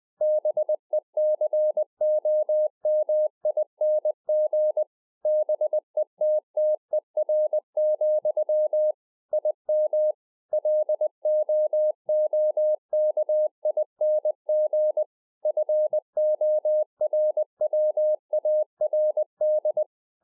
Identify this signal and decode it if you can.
Morse code, heard as a keyed tone, about 20 wpm, 610 Hz. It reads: BECOMING BETTER, IM LOOKING FORWARD